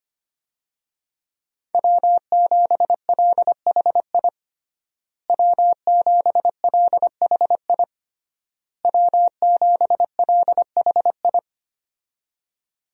Morse code, heard as a keyed tone, about 25 wpm, 700 Hz. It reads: W7LHI W7LHI W7LHI
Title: USA Call Signs - 5 Characters - Repeated 3x - 25wpm